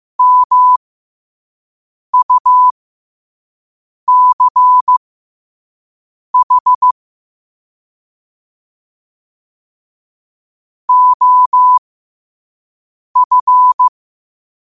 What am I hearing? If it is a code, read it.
Morse code, heard as a keyed tone, about 15 wpm, spaced as for 5 wpm, 1000 Hz. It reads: MUCH OF
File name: qotdfiles / 5wpm